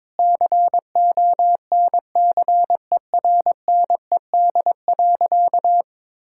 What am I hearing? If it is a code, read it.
Morse code, heard as a keyed tone, about 22 wpm, 700 Hz. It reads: CONCERNED.
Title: Sentences from Top 1000 Words - Review Part 6 - 22wpm